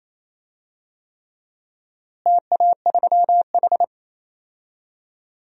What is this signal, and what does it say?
Morse code, heard as a keyed tone, about 28 wpm, 700 Hz. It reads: TA3H
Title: DXCC-100 Call Signs - 4 Characters - Repeated 3x - 28wpm